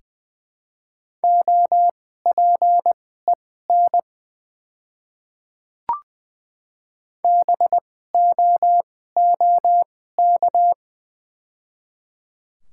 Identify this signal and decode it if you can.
Morse code, heard as a keyed tone, about 20 wpm, spaced as for 15 wpm, 700 Hz. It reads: OPEN BOOK
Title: 2x Character Spacing - Top 300 Words - 20wpm